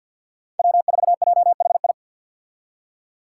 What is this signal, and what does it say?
Morse code, heard as a keyed tone, about 52 wpm, 700 Hz. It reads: W4JHI
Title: Five Challenging Letter-Number Combinations 50wpm